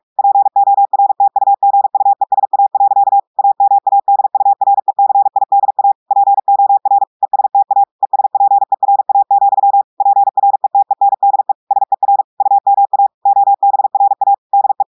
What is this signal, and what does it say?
Morse code, heard as a keyed tone, about 46 wpm, 800 Hz. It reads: PORTUGUESA. AMADURECIDA POR ESTA ESPERA, PRETENDE SER UMA OBRA DE
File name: portugais texte en clair 1 vitesse 45 mots minutes